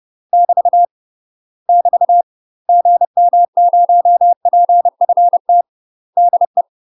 Morse code, heard as a keyed tone, about 30 wpm, 700 Hz. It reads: <BT> <BT> GM0PFT DE
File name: UK MASTER QSOs TRAINING / 30WPM / UK MASTER QSOs 0046